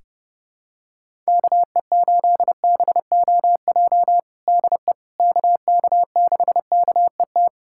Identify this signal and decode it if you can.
Morse code, heard as a keyed tone, about 32 wpm, 700 Hz. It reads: KE8BOJ DE KK6KET
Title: USA Callsign DE Callsign - 6 Characters - 30wpm